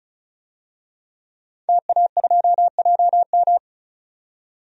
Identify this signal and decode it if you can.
Morse code, heard as a keyed tone, about 34 wpm, 700 Hz. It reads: TA2JM
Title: DXCC-100 Call Signs - 5 Characters - 35wpm